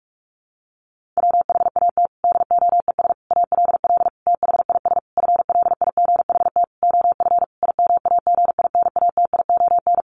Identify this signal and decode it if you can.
Morse code, heard as a keyed tone, about 44 wpm, 700 Hz. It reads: WHAT DOES ALL THIS FLIGHT OF IMAGINATION
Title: Sentences from Top 1300 Words 45wpm